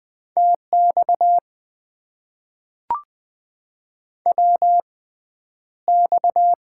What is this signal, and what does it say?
Morse code, heard as a keyed tone, about 20 wpm, spaced as for 10 wpm, 700 Hz. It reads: TX WX
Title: Top 90 QSO Elements - 6x Character Spacing - 20wpm